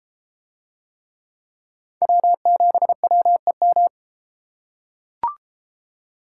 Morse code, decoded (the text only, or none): W7WEM